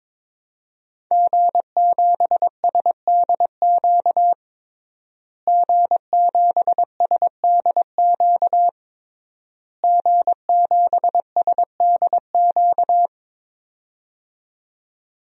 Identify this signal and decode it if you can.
Morse code, heard as a keyed tone, about 21 wpm, 700 Hz. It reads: G7SDQ G7SDQ G7SDQ
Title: DXCC-100 Call Signs - 5 Characters - Repeated 3x - 22wpm